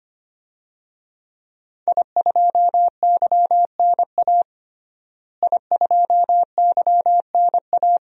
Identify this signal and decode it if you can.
Morse code, heard as a keyed tone, about 25 wpm, 700 Hz. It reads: I2YNA I2YNA
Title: DXCC-100 Call Signs - 5 Characters - Repeated 2x - 25wpm